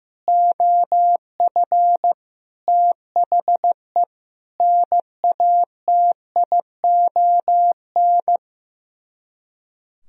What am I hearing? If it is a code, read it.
Morse code, heard as a keyed tone, about 15 wpm, 700 Hz. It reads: OF THE NATION
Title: Speed-Racing - Sets of 5 Words 15wpm